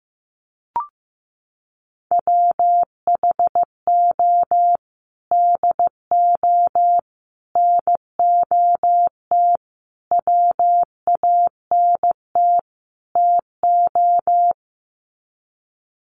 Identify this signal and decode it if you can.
Morse code, heard as a keyed tone, about 15 wpm, 700 Hz. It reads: WHO DO NOT WANT TO
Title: Sets of 5 Words Encore2 15wpm